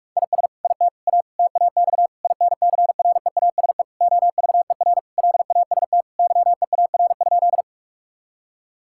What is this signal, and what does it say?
Morse code, heard as a keyed tone, about 46 wpm, 700 Hz. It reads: IS IT A TAX INCREASE OVER LAST YEAR?
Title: Sentences from Top 1000 Words - Review Part 5 - 45wpm